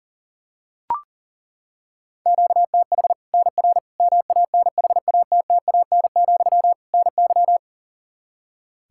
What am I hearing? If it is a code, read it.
Morse code, heard as a keyed tone, about 40 wpm, 700 Hz. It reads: QTH NR MANHATTAN, NY
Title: Practice QSO - Exchange 1 - Beginning - 40wpm